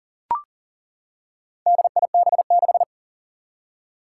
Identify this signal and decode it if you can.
Morse code, heard as a keyed tone, about 40 wpm, 700 Hz. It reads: DIB6
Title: Four Challenging Letter-Number Combinations 40wpm